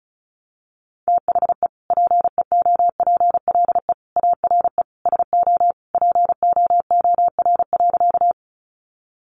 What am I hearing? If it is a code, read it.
Morse code, heard as a keyed tone, about 34 wpm, 700 Hz. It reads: THE PEOPLE ARE SO POOR.
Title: Sentences from Top 1000 Words 35wpm